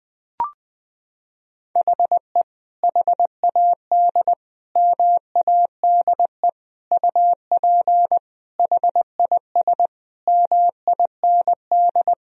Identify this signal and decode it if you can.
Morse code, heard as a keyed tone, about 20 wpm, 700 Hz. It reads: HE HAD MADE UP HIS MIND
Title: Sentences from Top 1000 Words - Review Part 3 - 20wpm